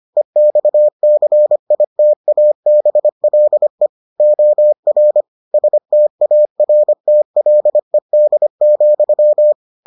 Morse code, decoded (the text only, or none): EXCITABLE OR STARTLED,